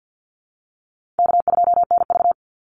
Morse code, heard as a keyed tone, about 48 wpm, 700 Hz. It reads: K2NV
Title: Call Signs Repeated 2x - 50wpm